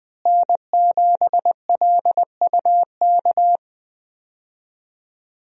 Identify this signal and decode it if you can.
Morse code, heard as a keyed tone, about 20 wpm, 700 Hz. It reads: N7LUK
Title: USA Call Signs - 5 Characters - Repeated 3x - 20wpm